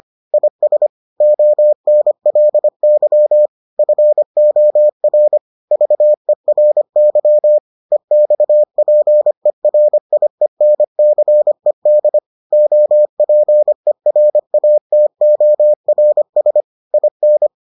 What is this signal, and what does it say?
Morse code, heard as a keyed tone, about 25 wpm, 600 Hz. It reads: IS ONLY FOR VERY EXPERIENCED OPERATORS IN